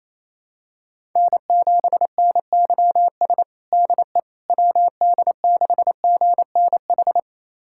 Morse code, heard as a keyed tone, about 27 wpm, 700 Hz. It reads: N7NYS DE WD6GNH
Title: USA Callsign DE Callsign 28wpm